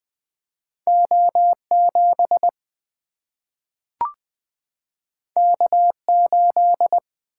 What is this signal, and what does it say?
Morse code, heard as a keyed tone, about 20 wpm, 700 Hz. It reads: O7 K8
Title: Single-Letter Single-Number 20wpm